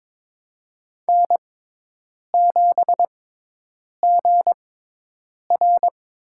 Morse code, heard as a keyed tone, about 22 wpm, spaced as for 8 wpm, 700 Hz. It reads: N7GR
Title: USA Call Signs - 4 Characters - 6x Character Spacing - 22wpm